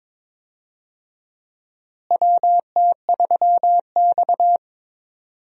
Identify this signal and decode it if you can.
Morse code, heard as a keyed tone, about 22 wpm, 700 Hz. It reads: WT3X
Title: USA Call Signs - 4 Characters - Repeated 2x - 22wpm